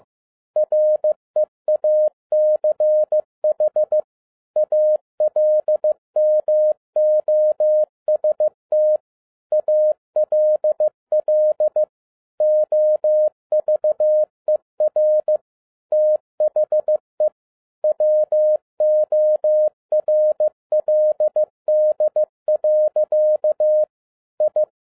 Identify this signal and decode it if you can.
Morse code, heard as a keyed tone, about 15 wpm, 610 Hz. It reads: REACH ALMOST ALL OVER THE WORLD. I
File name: 2011-07-mp3 / n42 EXCITING QRP CW OPERATION 15wpm